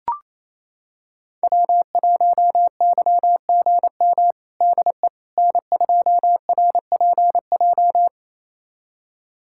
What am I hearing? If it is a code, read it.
Morse code, heard as a keyed tone, about 28 wpm, 700 Hz. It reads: W1YGM DE N2RPJ
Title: USA Callsign DE Callsign - 5 Characters - 28wpm